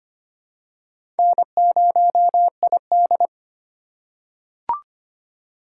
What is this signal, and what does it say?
Morse code, heard as a keyed tone, about 25 wpm, 700 Hz. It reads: N0ID
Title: USA Call Signs - 2x Character Spacing - 25wpm